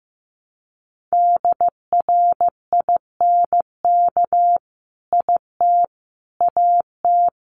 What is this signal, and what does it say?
Morse code, heard as a keyed tone, about 15 wpm, 700 Hz. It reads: DRINK IT AT
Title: Sentences from Top 1500 Words 15wpm